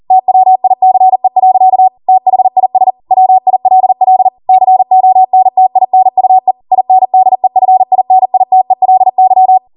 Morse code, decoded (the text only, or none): TWICE. THIS WILL CONTINUE INDEFINITELY